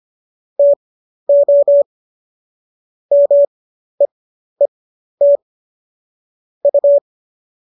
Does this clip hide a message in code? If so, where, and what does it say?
Yes: Morse code from 0.6 to 7.0 s, heard as a keyed tone, about 25 wpm, spaced as for 12 wpm, 570 Hz. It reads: TO MEET U